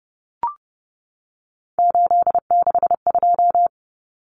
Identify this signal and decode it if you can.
Morse code, heard as a keyed tone, about 30 wpm, 700 Hz. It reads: 862
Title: Numbers - 3-Digit 30wpm